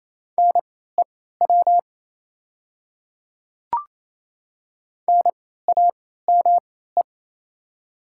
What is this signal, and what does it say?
Morse code, heard as a keyed tone, about 28 wpm, spaced as for 16 wpm, 700 Hz. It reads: NEW NAME
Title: 3x Character Spacing - Top 200 Words - 28wpm